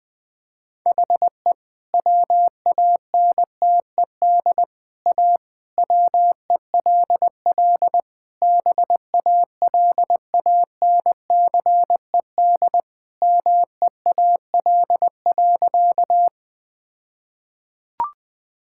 Morse code, heard as a keyed tone, about 20 wpm, 700 Hz. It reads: HE WANTED A WELL BALANCED MEAL.
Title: All-Age Appropriate Jokes - 20wpm